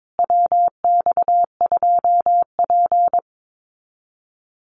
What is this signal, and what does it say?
Morse code, heard as a keyed tone, about 22 wpm, 700 Hz. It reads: WX2P
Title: USA Call Signs Repeated 3x 22wpm